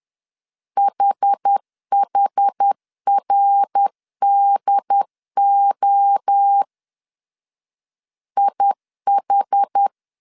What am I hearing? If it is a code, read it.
Morse code, heard as a keyed tone, about 10 wpm, 800 Hz. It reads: HHRDO IH
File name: CW / DET / 1 3 1500C S C 10wpm 3335